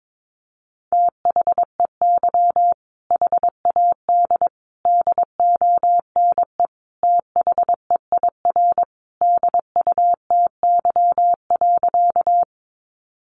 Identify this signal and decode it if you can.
Morse code, heard as a keyed tone, about 22 wpm, 700 Hz. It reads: THEY HAD DONE THEIR DUTY.
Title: Sentences from Top 1500 Words 22wpm